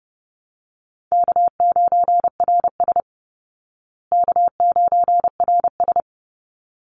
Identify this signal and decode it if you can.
Morse code, heard as a keyed tone, about 30 wpm, 700 Hz. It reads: K9RS K9RS
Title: Easy Call Signs Repeated 2x - 30wpm